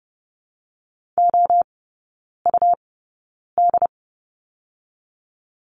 Morse code, heard as a keyed tone, about 30 wpm, spaced as for 9 wpm, 690 Hz. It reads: OUD
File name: Lesson 17e - Letter u - Three Characters 30wpm